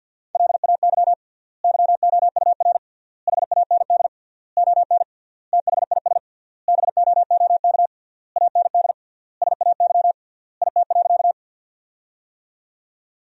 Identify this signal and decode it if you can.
Morse code, heard as a keyed tone, about 50 wpm, 700 Hz. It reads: LAY YOUR HAND ON THIS BOOK AND SAY IT.
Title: 3x Word-Spacing - Sentences from Top 400 Words - 50wpm